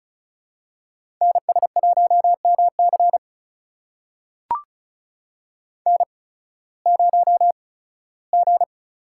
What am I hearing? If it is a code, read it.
Morse code, heard as a keyed tone, about 34 wpm, spaced as for 18 wpm, 700 Hz. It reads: NS1MC N0G
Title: USA Call Signs - 5 Characters - 8x Character Spacing - 35wpm